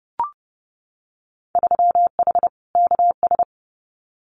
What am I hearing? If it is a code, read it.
Morse code, heard as a keyed tone, about 31 wpm, 700 Hz. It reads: 3H KS
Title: Winter Field Day Exchanges Encore 30wpm